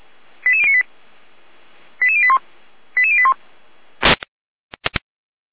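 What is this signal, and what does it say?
Five-tone selective call (ZVEI), heard as quick runs of five tones, tones of about 70 ms. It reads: 88098 88081 88081
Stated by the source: Five-tone ZVEI selective call received on SDR.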